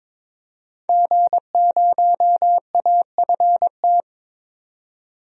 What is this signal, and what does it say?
Morse code, heard as a keyed tone, about 23 wpm, 700 Hz. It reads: G0AFT
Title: DXCC-100 Call Signs - 5 Characters - 22wpm